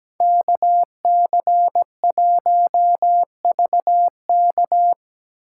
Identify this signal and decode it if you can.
Morse code, heard as a keyed tone, about 20 wpm, 700 Hz. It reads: KC1VK
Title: Speed Racing - USA 2x2 Call Signs 17wpm